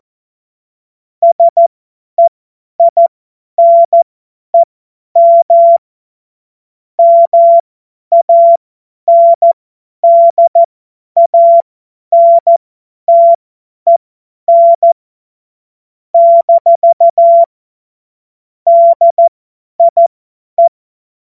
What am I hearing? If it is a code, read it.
Morse code, heard as a keyed tone, about 14 wpm, spaced as for 10 wpm, 680 Hz. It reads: SEINEM MANDANTEN - DIE
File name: Morsekurs / Witze / Witz 09. August 2022 50 F70